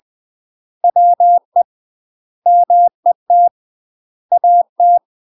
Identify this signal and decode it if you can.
Morse code, heard as a keyed tone, about 20 wpm, 700 Hz. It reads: WE MET AT
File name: UK MASTER QSOs TRAINING / 20WPM / UK MASTER QSOs 0064